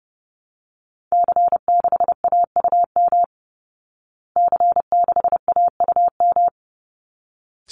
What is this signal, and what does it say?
Morse code, heard as a keyed tone, about 30 wpm, 700 Hz. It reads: C6AUM C6AUM
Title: Call Signs Repeated 3x - 30wpm